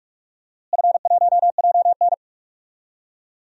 Morse code, heard as a keyed tone, about 43 wpm, 700 Hz. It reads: F1JN
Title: DXCC-100 Call Signs - 4 Characters - Repeated 2x - 45wpm